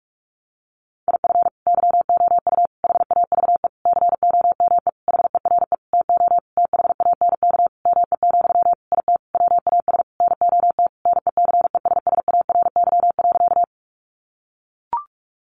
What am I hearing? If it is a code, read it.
Morse code, heard as a keyed tone, about 44 wpm, 700 Hz. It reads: IF YOU HAVE COME HERE TO THANK ME, IT WAS NOT NECESSARY.
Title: Sentences from Top 800 Words 45wpm